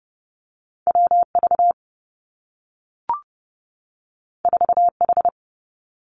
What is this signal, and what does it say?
Morse code, heard as a keyed tone, about 30 wpm, 700 Hz. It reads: W V 4 H
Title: Two Challenging Letter-Number Combinations 30wpm